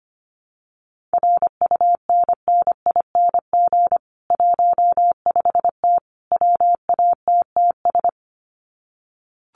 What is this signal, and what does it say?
Morse code, heard as a keyed tone, about 25 wpm, 700 Hz. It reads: RUNNING 15T WATTS
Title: Common QSO Phrases 25wpm